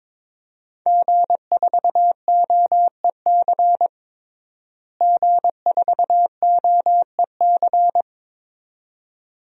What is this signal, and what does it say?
Morse code, heard as a keyed tone, about 23 wpm, 700 Hz. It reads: G4OEC G4OEC
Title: DXCC-100 Call Signs - 5 Characters - Repeated 2x - 22wpm